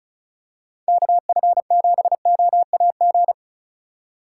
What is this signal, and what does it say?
Morse code, heard as a keyed tone, about 36 wpm, 700 Hz. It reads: KF7OAG
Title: USA Call Signs - 6 Characters - 2x Character Spacing - 35wpm